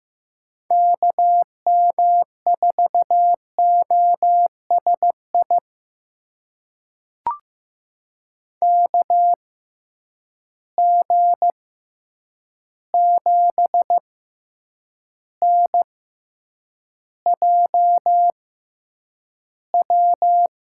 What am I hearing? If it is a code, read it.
Morse code, heard as a keyed tone, about 15 wpm, spaced as for 8 wpm, 700 Hz. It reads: KM4OSI KG7NJW
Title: USA Call Signs - 6 Characters - 6x Character Spacing - 15wpm